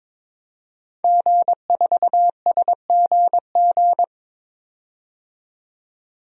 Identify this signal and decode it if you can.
Morse code, heard as a keyed tone, about 22 wpm, 700 Hz. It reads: G4SGG
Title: DXCC-100 Call Signs - 5 Characters - 22wpm